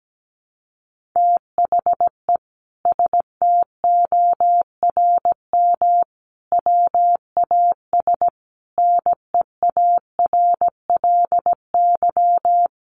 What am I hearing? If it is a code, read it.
Morse code, heard as a keyed tone, about 17 wpm, 700 Hz. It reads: THE STORM WAS NEARLY
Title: Sentences from Top 1500 Words 17wpm